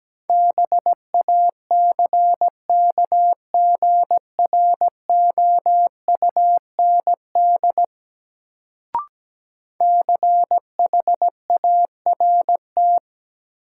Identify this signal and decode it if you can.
Morse code, heard as a keyed tone, about 17 wpm, 700 Hz. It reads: BACKGROUND CHART
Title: Sentences from Top 1300 Words - New Word Drill - 17wpm